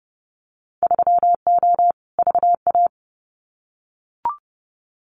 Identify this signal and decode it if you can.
Morse code, heard as a keyed tone, about 30 wpm, 700 Hz. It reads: 3O VA
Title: Winter Field Day Exchanges 30wpm